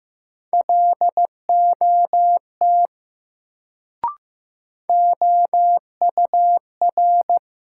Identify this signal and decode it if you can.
Morse code, heard as a keyed tone, about 15 wpm, 700 Hz. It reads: LOT OUR
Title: Sets of 2 Words - New Word Drill - 15wpm